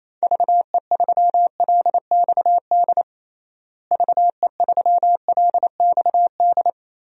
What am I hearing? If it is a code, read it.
Morse code, heard as a keyed tone, about 28 wpm, 700 Hz. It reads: VE3LXD VE3LXD
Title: DXCC-100 Call Signs - 6 Characters - Repeated 3x - 28wpm